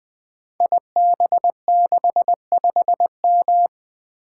Morse code, heard as a keyed tone, about 20 wpm, 700 Hz. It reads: IB65M
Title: Code Groups - 5 Letter:Numbers - 20wpm